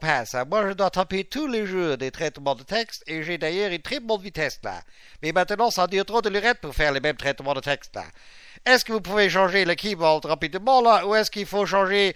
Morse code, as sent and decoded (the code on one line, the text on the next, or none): none